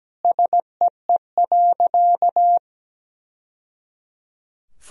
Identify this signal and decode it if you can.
Morse code, heard as a keyed tone, about 17 wpm, 700 Hz. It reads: SEE.
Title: Sentences from Top 1000 Words - Review Part 1 - 17wpm